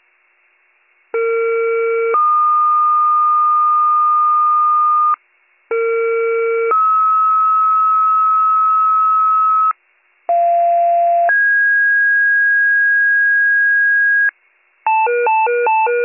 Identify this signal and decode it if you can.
Two-tone page, heard as a one-second tone, then a three-second tone, about 460 then 1220 Hz, about 460 then 1290 Hz, about 690 then 1650 Hz, with noise. It is heard from 1.1 to 14.3 s.